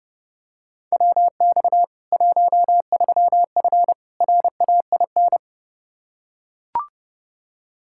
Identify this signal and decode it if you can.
Morse code, heard as a keyed tone, about 32 wpm, 700 Hz. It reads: WX 13F RAIN
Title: Practice QSO - Exchange 2 - Info - 30wpm